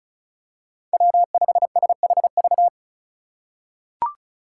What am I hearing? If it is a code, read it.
Morse code, heard as a keyed tone, about 34 wpm, 700 Hz. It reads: W5SHV
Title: USA Call Signs - 5 Characters - 3x Character Spacing - 35wpm